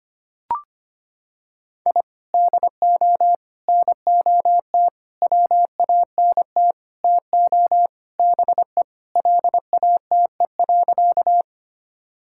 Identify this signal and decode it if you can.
Morse code, heard as a keyed tone, about 25 wpm, 700 Hz. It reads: I DO NOT WANT TO BE LATE.
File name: practice / 25wpm / Sentences with 4-Word Phrases - 25wpm